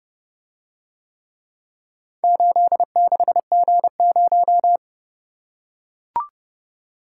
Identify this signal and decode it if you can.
Morse code, heard as a keyed tone, about 31 wpm, 700 Hz. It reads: 86G0